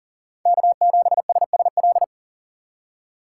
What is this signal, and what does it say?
Morse code, heard as a keyed tone, about 40 wpm, 700 Hz. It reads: K7SSL
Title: USA Call Signs - 2x Character Spacing - 40wpm